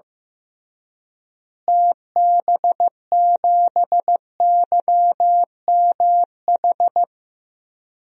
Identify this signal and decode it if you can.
Morse code, heard as a keyed tone, about 15 wpm, 700 Hz. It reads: TB7YMH
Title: DXCC-100 Call Signs - 6 Characters - Repeated 3x - 15wpm